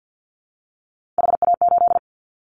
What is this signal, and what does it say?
Morse code, heard as a keyed tone, about 50 wpm, 700 Hz. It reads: HA8
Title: Callsign Prefixes for Top 50 DXCC - 50wpm